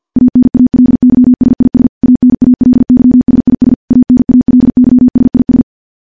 Morse code, heard as a keyed tone, about 50 wpm, 250 Hz. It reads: ANALYSIS ANALYSIS ANALYSIS
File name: morse-training-50WPM